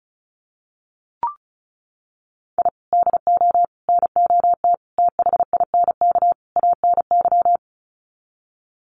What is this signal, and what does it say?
Morse code, heard as a keyed tone, about 34 wpm, 700 Hz. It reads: I DO NOT THINK ANY
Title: Sets of 5 Words Encore 35wpm